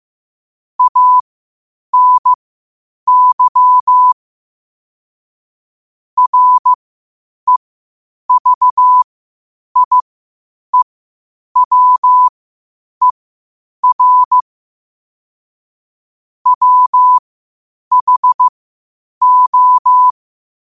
Morse code, heard as a keyed tone, about 15 wpm, spaced as for 8 wpm, 1000 Hz. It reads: ANY REVIEWER WHO
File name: qotdfiles / 8wpm